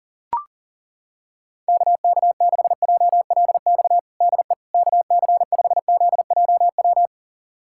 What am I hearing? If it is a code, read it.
Morse code, heard as a keyed tone, about 40 wpm, 700 Hz. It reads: KK6JLX DE KC5ZJW
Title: USA Callsign DE Callsign 40wpm